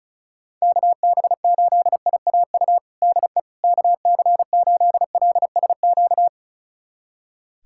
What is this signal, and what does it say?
Morse code, heard as a keyed tone, about 36 wpm, 700 Hz. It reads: KB8IAU DE KC8LSQ
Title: USA Callsign DE Callsign - 6 Characters - 35wpm